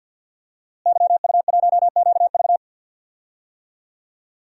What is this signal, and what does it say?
Morse code, heard as a keyed tone, about 52 wpm, 700 Hz. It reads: YU1QV
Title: DXCC-100 Call Signs - 5 Characters - Repeated 2x - 50wpm